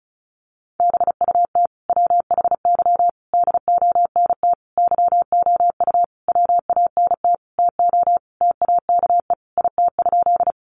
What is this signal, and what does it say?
Morse code, heard as a keyed tone, about 34 wpm, 700 Hz. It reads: BUT WHY DONT YOU WANT TO TAKE IT?
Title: Sentences from Top 300 Words 35wpm